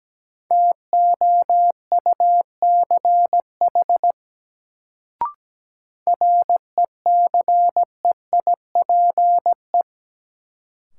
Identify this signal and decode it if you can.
Morse code, heard as a keyed tone, about 17 wpm, 700 Hz. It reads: TOUCH RECEIPE
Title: Sentences from Top 1200 Words - New Word Drill Alt - 17wpm